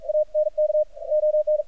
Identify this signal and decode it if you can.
Morse code, heard as a keyed tone, about 47 wpm, 610 Hz. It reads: ANK ON
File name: archived / CRSnet